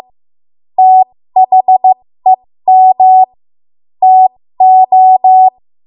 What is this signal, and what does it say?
Morse code, heard as a keyed tone, about 15 wpm, spaced as for 13 wpm, 750 Hz. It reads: THEM TO
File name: Morse / 13WPM